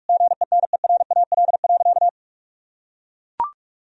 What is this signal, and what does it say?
Morse code, heard as a keyed tone, about 41 wpm, 700 Hz. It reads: GENERAL.
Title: Sentences from Top 1000 Words - Review Part 3 - 45wpm